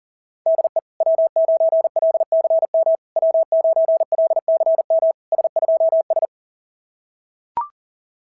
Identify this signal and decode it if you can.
Morse code, heard as a keyed tone, about 37 wpm, 650 Hz. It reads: DE W9LCM W9LCM S2S
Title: SOTA QSO Practice 40wpm